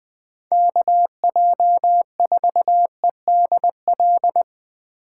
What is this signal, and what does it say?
Morse code, heard as a keyed tone, about 20 wpm, 700 Hz. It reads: KJ4EDL